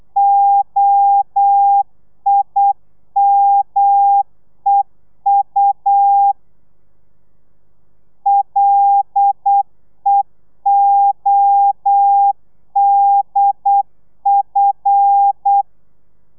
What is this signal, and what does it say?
Morse code, heard as a keyed tone, about 8 wpm, 800 Hz. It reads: OIMEU LEODF
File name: telegrafia / i2tfj / Lez5 35